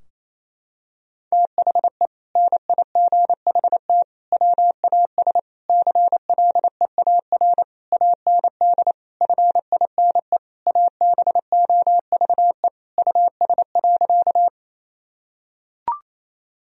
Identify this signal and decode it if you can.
Morse code, heard as a keyed tone, about 28 wpm, 700 Hz. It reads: THE NIGHT WAS CLEAR AND FINE ABOVE US.
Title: Sentences from Top 1000 Words - Review Part 2 - 28wpm